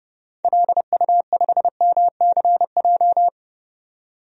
Morse code, heard as a keyed tone, about 30 wpm, 700 Hz. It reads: LU5MCJ
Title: DXCC-100 Call Signs - 6 Characters - 30wpm